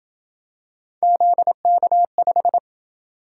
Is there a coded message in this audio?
Yes: Morse code from 1.0 to 2.6 s, heard as a keyed tone, about 27 wpm, 700 Hz. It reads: ZK5